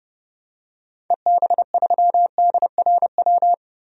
Morse code, heard as a keyed tone, about 30 wpm, 700 Hz. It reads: EB3DRW
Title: DXCC-100 Call Signs - 6 Characters - 30wpm